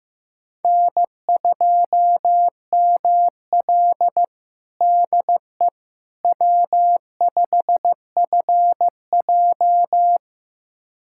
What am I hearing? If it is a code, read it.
Morse code, heard as a keyed tone, about 15 wpm, 700 Hz. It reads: N2ML DE W5FJ
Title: USA Callsign DE Callsign - 4 Characters - 15wpm